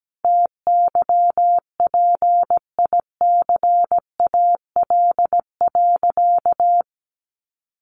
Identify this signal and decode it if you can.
Morse code, heard as a keyed tone, about 17 wpm, 700 Hz. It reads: TYPICAL.
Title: Sentences from Top 1100 Words 17wpm